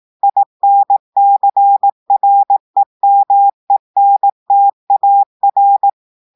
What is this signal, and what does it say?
Morse code, heard as a keyed tone, about 18 wpm, 800 Hz. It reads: INCREMENTAR
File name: audios / portugais / portugais texte en clair 2 vitesse 18 mots minutes